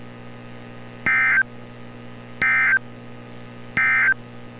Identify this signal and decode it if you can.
Emergency Alert System (SAME) tones, heard as harsh buzzing bursts. It is heard from 0.9 to 4.2 s.
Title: NOAA Weather Radio - Altoona, PA